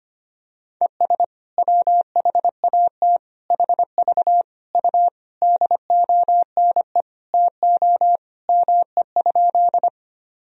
Morse code, heard as a keyed tone, about 25 wpm, 700 Hz. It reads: ES WHAT HV U DONE TO ME?
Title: Sentences from Top 500 Words and Abbreviations 25wpm